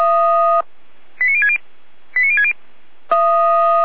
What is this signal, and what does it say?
Five-tone selective call (ZVEI), heard as quick runs of five tones, tones of about 70 ms. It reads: 89060 89060